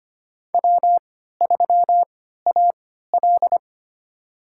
Morse code, heard as a keyed tone, about 25 wpm, spaced as for 14 wpm, 700 Hz. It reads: W3AL
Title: USA Call Signs - 4 Characters - 3x Character Spacing - 25wpm